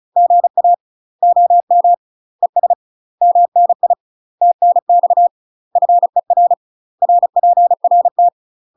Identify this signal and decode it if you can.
Morse code, heard as a keyed tone, about 34 wpm, 700 Hz. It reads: GA OM ES MNI TNX FER RPRT
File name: UK MASTER QSOs TRAINING / 35WPM / UK MASTER QSOs 0043